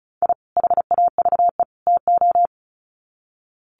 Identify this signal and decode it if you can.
Morse code, heard as a keyed tone, about 34 wpm, 700 Hz. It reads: I HAVE TO
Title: Sets of 3 Words 35wpm